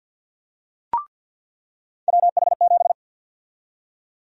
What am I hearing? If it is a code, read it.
Morse code, heard as a keyed tone, about 47 wpm, 700 Hz. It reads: WH7
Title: USA Call Sign Prefixes - 50wpm